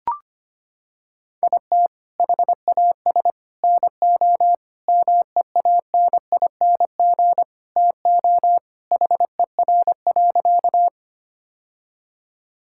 Morse code, heard as a keyed tone, about 25 wpm, 700 Hz. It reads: IT HAS NO MEANING TO HER.
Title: Sentences from Top 1000 Words - Review Part 6 - 25wpm